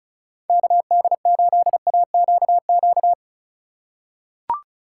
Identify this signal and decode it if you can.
Morse code, heard as a keyed tone, about 36 wpm, 700 Hz. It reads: KD8AQQ
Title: USA Call Signs - 6 Characters - 4x Character Spacing - 35wpm